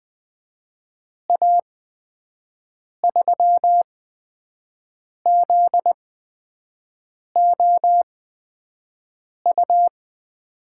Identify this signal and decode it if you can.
Morse code, heard as a keyed tone, about 20 wpm, spaced as for 5 wpm, 700 Hz. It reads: A3ZOU
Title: USA Call Signs - 8x Character Spacing - 20wpm